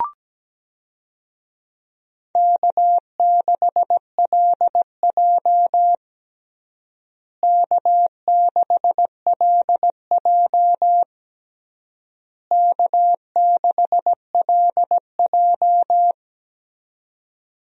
Morse code, heard as a keyed tone, about 17 wpm, 700 Hz. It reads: K6LJ K6LJ K6LJ
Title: USA Call Signs - 4 Characters - Repeated 3x - 17wpm